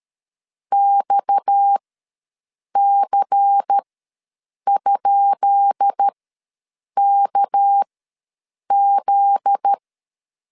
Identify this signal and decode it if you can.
Morse code, heard as a keyed tone, about 13 wpm, spaced as for 7 wpm, 800 Hz. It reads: XC?KZ
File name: LEZIONI / 10 SIGNS / 10B INT ws 12wpm 3527